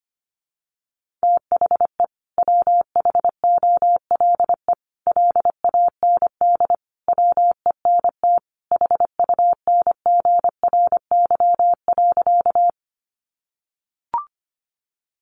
Morse code, heard as a keyed tone, about 25 wpm, 700 Hz. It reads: THE WHOLE LAND WENT HUNGRY.
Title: Sentences from Top 1400 Words 25wpm